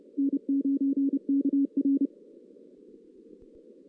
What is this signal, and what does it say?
Morse code, heard as a keyed tone, about 30 wpm, 280 Hz. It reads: N9KR
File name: 2016 JAN CW NAQP